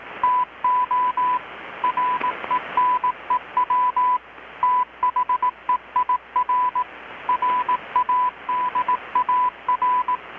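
Morse code, heard as a keyed tone, about 18 wpm, 1000 Hz, with radio noise. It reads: TO RENEW THEIR RADAR
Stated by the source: Morse from KPH's Night of Nights XX maritime radiotelegraphy event, received in Chiapas, Mexico about 3,800 km away on a Kenwood R-600 with a grounded 10 m long-wire antenna.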